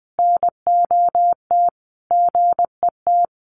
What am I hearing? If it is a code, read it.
Morse code, heard as a keyed tone, about 20 wpm, 700 Hz. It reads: NOT GET
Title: Sets of 5 Words Encore2 20wpm